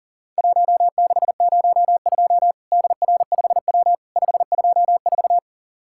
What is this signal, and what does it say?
Morse code, heard as a keyed tone, about 44 wpm, 700 Hz. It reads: 1602 DR5W 524
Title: Worked All Europe DX Contest - QTC Exchange - 40wpm